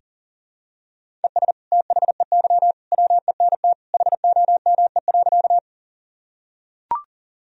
Morse code, heard as a keyed tone, about 40 wpm, 700 Hz. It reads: ES THEY WENT HOME.
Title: Sentences from Top 200 Words and Abbreviations 40wpm